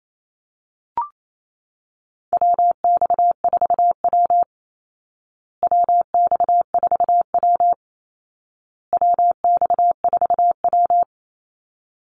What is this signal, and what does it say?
Morse code, heard as a keyed tone, about 28 wpm, 700 Hz. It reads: WX4W WX4W WX4W
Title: Call Signs Repeated 2x - 28wpm